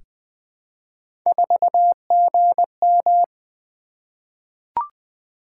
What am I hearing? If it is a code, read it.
Morse code, heard as a keyed tone, about 20 wpm, 700 Hz. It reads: 4GM
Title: Three Letter-Number Combinations - 20wpm